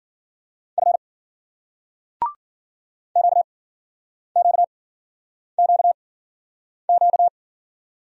Morse code, heard as a keyed tone, about 52 wpm, 700 Hz. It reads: F Q Q Q Q